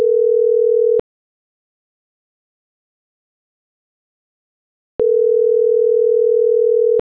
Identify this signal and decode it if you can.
Ringback tone (North America), 440 + 480 Hz.